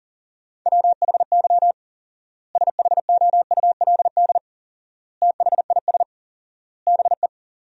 Morse code, heard as a keyed tone, about 40 wpm, 700 Hz. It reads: WHY SHOULD THIS BE
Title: 4x Word-Spacing - Sentences from Top 300 Words - 40wpm